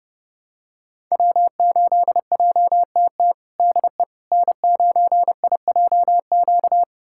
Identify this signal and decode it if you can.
Morse code, heard as a keyed tone, about 30 wpm, 700 Hz. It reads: W8JTT DE N9IJQ
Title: USA Callsign DE Callsign - 5 Characters - 30wpm